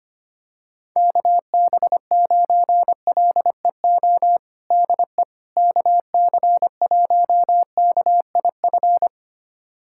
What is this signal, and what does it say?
Morse code, heard as a keyed tone, about 25 wpm, 700 Hz. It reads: KB9LEO DE KC1KIF
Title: USA Callsign DE Callsign - 6 Characters - 25wpm